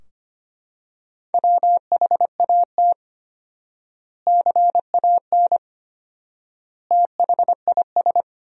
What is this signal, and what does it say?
Morse code, heard as a keyed tone, about 25 wpm, 700 Hz. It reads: WHAT CAN THIS
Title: 4x Word-Spacing - Sentences from Top 100 Words - 25wpm